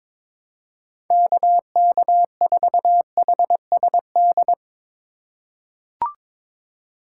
Morse code, heard as a keyed tone, about 22 wpm, 700 Hz. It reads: KK4HSD
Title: USA Call Signs - 3x Character Spacing - 22wpm